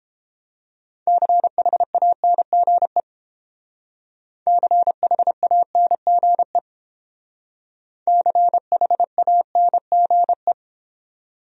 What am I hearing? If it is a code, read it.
Morse code, heard as a keyed tone, about 30 wpm, 700 Hz. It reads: CHANGE CHANGE CHANGE